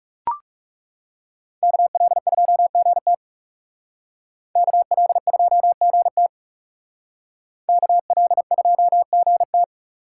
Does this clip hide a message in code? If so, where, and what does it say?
Yes: Morse code from 1.6 to 9.6 s, heard as a keyed tone, about 38 wpm, 700 Hz. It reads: KL2GT KL2GT KL2GT